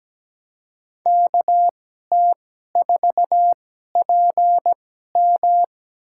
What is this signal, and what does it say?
Morse code, heard as a keyed tone, about 17 wpm, spaced as for 12 wpm, 700 Hz. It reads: KT4PM
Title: USA Call Signs - 5 Characters - 2x Character Spacing - 17wpm